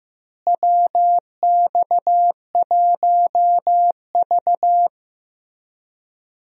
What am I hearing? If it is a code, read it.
Morse code, heard as a keyed tone, about 16 wpm, 700 Hz. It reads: WX1V
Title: USA Call Signs - 4 Characters - Repeated 3x - 15wpm